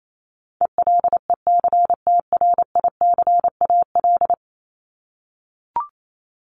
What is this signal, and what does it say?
Morse code, heard as a keyed tone, about 28 wpm, 700 Hz. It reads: ELECTRICAL
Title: Top 751-1000 Words 28wpm